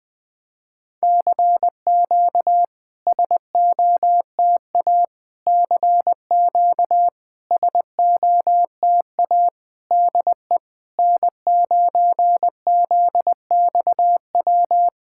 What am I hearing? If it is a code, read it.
Morse code, heard as a keyed tone, about 21 wpm, 700 Hz. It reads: CQ SOTA CQ SOTA DE N9ZXW
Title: SOTA QSO Practice 20wpm